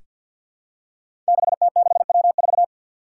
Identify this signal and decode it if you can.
Morse code, heard as a keyed tone, about 47 wpm, 700 Hz. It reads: 6T6W4
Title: Five Challenging Letter-Number Combinations 50wpm